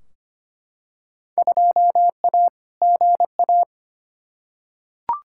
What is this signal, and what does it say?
Morse code, heard as a keyed tone, about 25 wpm, 700 Hz. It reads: 2A GA
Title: Field Day Exchanges Encore - 25wpm